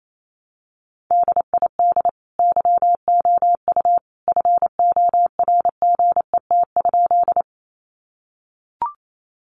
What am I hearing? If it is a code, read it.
Morse code, heard as a keyed tone, about 28 wpm, 700 Hz. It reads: DID YOU FORGET?
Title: Sentences from Top 1200 Words 28wpm